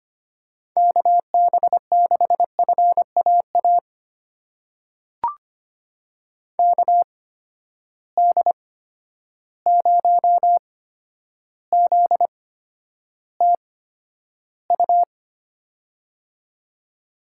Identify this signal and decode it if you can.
Morse code, heard as a keyed tone, about 25 wpm, spaced as for 11 wpm, 700 Hz. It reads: KB6FAA KD0ZTU
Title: USA Call Signs - 6 Characters - 8x Character Spacing - 25wpm